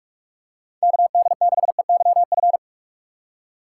Morse code, heard as a keyed tone, about 46 wpm, 700 Hz. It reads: KD6EYF
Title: USA Call Signs - 6 Characters - 4x Character Spacing - 45wpm